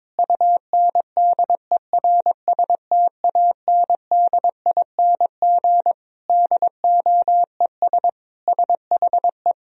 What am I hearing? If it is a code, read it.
Morse code, heard as a keyed tone, about 22 wpm, 700 Hz. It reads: UNDERSTANDING DOES SHE
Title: Sentences from Top 1000 Words - Review Part 3 - 22wpm